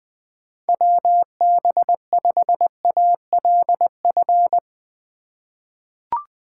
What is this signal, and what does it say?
Morse code, heard as a keyed tone, about 20 wpm, 700 Hz. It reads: WB5ALF
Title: USA Call Signs - 4x Character Spacing - 20wpm